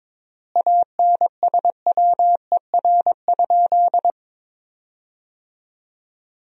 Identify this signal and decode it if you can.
Morse code, heard as a keyed tone, about 22 wpm, 700 Hz. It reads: ANSWER?
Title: 4x Word-Spacing - Sentences from Top 300 Words - 22wpm